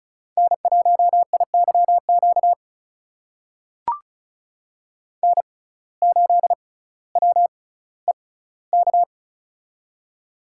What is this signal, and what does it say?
Morse code, heard as a keyed tone, about 36 wpm, spaced as for 18 wpm, 700 Hz. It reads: N1IYQ N8WEK
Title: USA Call Signs - 5 Characters - 6x Character Spacing - 35wpm